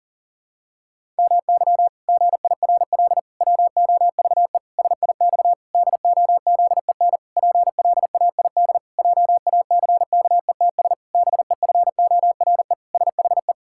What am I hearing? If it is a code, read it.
Morse code, heard as a keyed tone, about 40 wpm, 700 Hz. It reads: MY GIRL WOVE SIX DOZEN PLAID JACKETS BEFORE SHE
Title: Pangram Challenge v2 - 40wpm